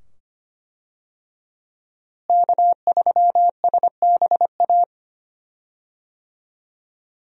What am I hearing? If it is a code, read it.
Morse code, heard as a keyed tone, about 25 wpm, 690 Hz. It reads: K3SBA
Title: USA Call Signs - 5 Characters - Repeated 2x - 25wpm